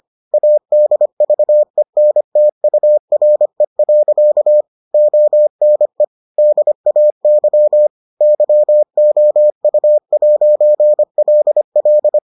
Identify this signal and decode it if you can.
Morse code, heard as a keyed tone, about 25 wpm, 600 Hz. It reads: ADVENTURE. ONE DAY YOU'LL